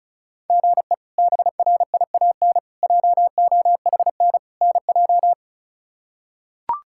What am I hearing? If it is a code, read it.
Morse code, heard as a keyed tone, about 34 wpm, 700 Hz. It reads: GE BRIAN JOHN NJ
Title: K1USN SST QSO Practice - 35wpm